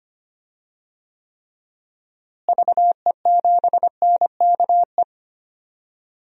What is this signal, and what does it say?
Morse code, heard as a keyed tone, about 25 wpm, 700 Hz. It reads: VE7NKE
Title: DXCC-100 Call Signs - 6 Characters - Repeated 3x - 25wpm